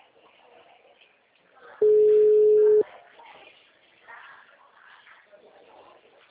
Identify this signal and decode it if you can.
Ringback tone (Europe, 425 Hz).